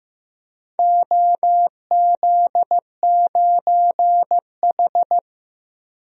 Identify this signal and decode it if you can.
Morse code, heard as a keyed tone, about 16 wpm, 700 Hz. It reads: OZ9H
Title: DXCC-100 Call Signs - 4 Characters - 15wpm